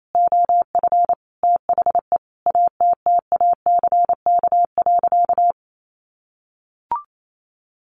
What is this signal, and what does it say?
Morse code, heard as a keyed tone, about 28 wpm, 700 Hz. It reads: OF THE ATTACK.
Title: Sentences from Top 1500 Words 28wpm